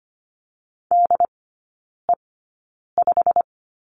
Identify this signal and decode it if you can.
Morse code, heard as a keyed tone, about 25 wpm, spaced as for 9 wpm, 700 Hz. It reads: DE5
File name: Lesson 15e - Number 5 - Three Characters 25wpm